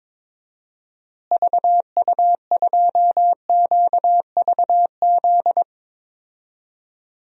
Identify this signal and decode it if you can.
Morse code, heard as a keyed tone, about 22 wpm, 700 Hz. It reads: VU2QVZ
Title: DXCC-100 Call Signs - 6 Characters - Repeated 3x - 22wpm